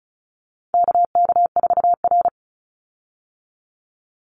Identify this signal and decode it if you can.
Morse code, heard as a keyed tone, about 34 wpm, 700 Hz. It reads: KK4R
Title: Call Signs Repeated 2x - 35wpm